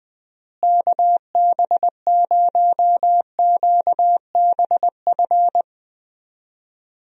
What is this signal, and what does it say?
Morse code, heard as a keyed tone, about 20 wpm, 700 Hz. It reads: KB0QBF
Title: USA Call Signs - 6 Characters - Repeated 2x - 20wpm